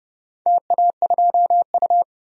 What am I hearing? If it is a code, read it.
Morse code, heard as a keyed tone, about 30 wpm, 700 Hz. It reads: TA2U
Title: DXCC-100 Call Signs - 4 Characters - 30wpm